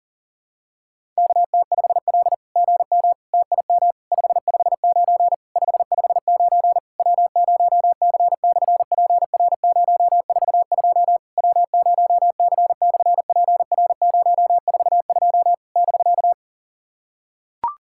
Morse code, heard as a keyed tone, about 40 wpm, 700 Hz. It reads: KT5L GM TIM 559 559 W0C/PR042 W0C/PR042 <BK>
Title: SOTA QSO Practice 40wpm